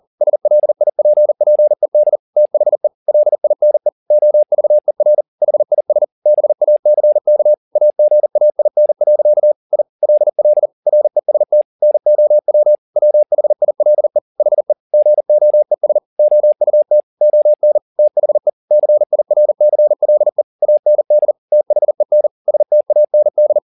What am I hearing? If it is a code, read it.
Morse code, heard as a keyed tone, about 40 wpm, 600 Hz. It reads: SLIPPED THE LINE OVER HIS BACK AGAIN. I LL REST NOW WHILE HE GOES OUT ON THE CIRCLE AND THEN STAND
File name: the old man and the sea 2 40wpm